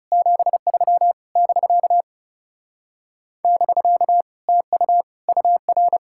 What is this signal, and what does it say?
Morse code, heard as a keyed tone, about 37 wpm, 700 Hz. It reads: 73 <BK> <BK> TU UR
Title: Speed-Racing Common QSO Phrases - 30wpm